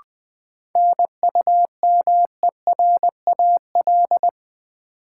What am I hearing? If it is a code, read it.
Morse code, heard as a keyed tone, about 20 wpm, 700 Hz. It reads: NUMERAL
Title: Sentences from Top 1000 Words - Review Part 2 - New Word Drill - 20wpm